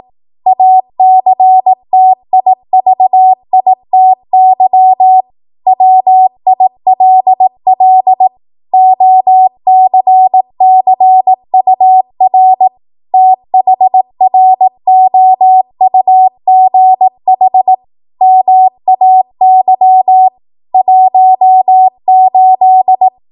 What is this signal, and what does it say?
Morse code, heard as a keyed tone, about 18 wpm, 750 Hz. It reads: ACTIVITY WILL OCCUR THROUGH MAY 18
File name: Morse / 18WPM